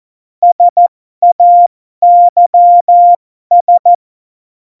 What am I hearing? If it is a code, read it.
Morse code, heard as a keyed tone, about 14 wpm, spaced as for 12 wpm, 700 Hz. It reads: SAYS
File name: Morsekurs / Witze